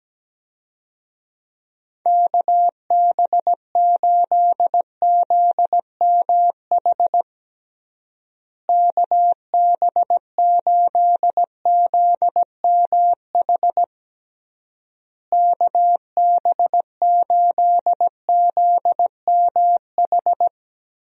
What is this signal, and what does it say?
Morse code, heard as a keyed tone, about 17 wpm, 700 Hz. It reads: KB8ZMH KB8ZMH KB8ZMH
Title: USA Call Signs - 6 Characters - Repeated 3x - 17wpm